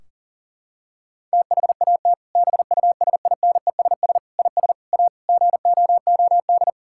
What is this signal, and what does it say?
Morse code, heard as a keyed tone, about 40 wpm, 700 Hz. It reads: THAT BUSINESS IS A GOOD
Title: Sentences from Top 1600 Words 40wpm